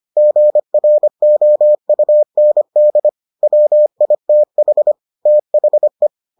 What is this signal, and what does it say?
Morse code, heard as a keyed tone, about 25 wpm, 600 Hz. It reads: GROUND WITH THE